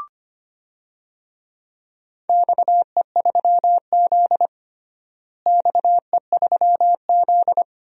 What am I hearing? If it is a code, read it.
Morse code, heard as a keyed tone, about 25 wpm, 700 Hz. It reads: XE3Z XE3Z
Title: DXCC-100 Call Signs - 4 Characters - Repeated 2x - 25wpm